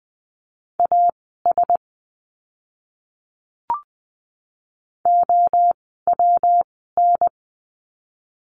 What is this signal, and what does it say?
Morse code, heard as a keyed tone, about 20 wpm, spaced as for 15 wpm, 700 Hz. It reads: AS OWN
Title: 146 Lesson 24f - Letter y - Words 20wpm